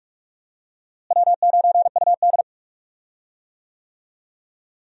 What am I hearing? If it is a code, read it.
Morse code, heard as a keyed tone, about 42 wpm, 700 Hz. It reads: W9UD
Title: USA Call Signs - 4 Characters - Repeated 3x - 45wpm